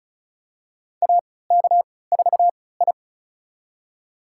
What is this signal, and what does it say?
Morse code, heard as a keyed tone, about 36 wpm, spaced as for 20 wpm, 700 Hz. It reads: AK4I
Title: USA Call Signs - 3x Character Spacing - 35wpm